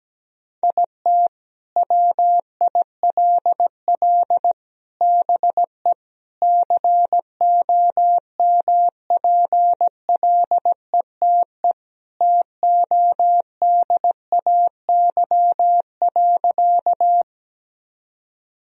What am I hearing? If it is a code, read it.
Morse code, heard as a keyed tone, about 17 wpm, 700 Hz. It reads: IT WILL BE COMPLETE TODAY.
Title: Sentences from Top 1000 Words - Review Part 2 - 17wpm